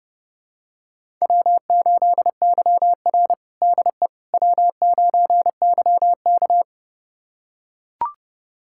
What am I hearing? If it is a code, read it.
Morse code, heard as a keyed tone, about 32 wpm, 700 Hz. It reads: W8YR DE W9YK
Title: USA Callsign DE Callsign - 4 Characters - 30wpm